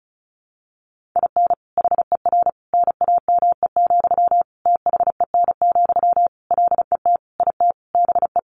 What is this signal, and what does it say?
Morse code, heard as a keyed tone, about 34 wpm, 700 Hz. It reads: IN HER NAME, THEN, LET IT BE
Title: Sentences from Top 500 Words 35wpm